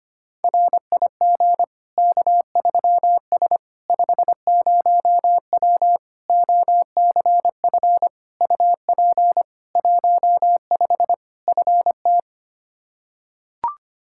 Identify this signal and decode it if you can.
Morse code, heard as a keyed tone, about 25 wpm, 700 Hz. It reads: RIG K3S 50W OCF UP 15 FT
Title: Practice QSO - Exchange 2 - Info - 25wpm